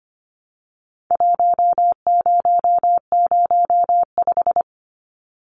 Morse code, heard as a keyed tone, about 25 wpm, 700 Hz. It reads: 1005
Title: Numbers - Contest Style - 4-Digit 25wpm